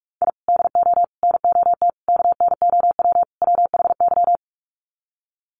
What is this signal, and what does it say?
Morse code, heard as a keyed tone, about 44 wpm, 700 Hz. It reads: I DO NOT KNOW WHY
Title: Sets of 5 Words 45wpm